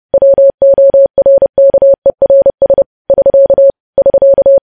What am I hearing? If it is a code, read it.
Morse code, heard as a keyed tone, about 30 wpm, 550 Hz. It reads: WORKERS <SK> <SK>
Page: news headlines in morse code at 30 wpm